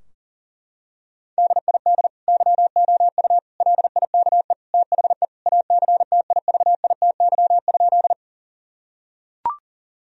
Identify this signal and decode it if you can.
Morse code, heard as a keyed tone, about 40 wpm, 700 Hz. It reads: DID YOU LIKE THE ACTIVITY?
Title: Sentences from Top 1000 Words - Review Part 4 - 40wpm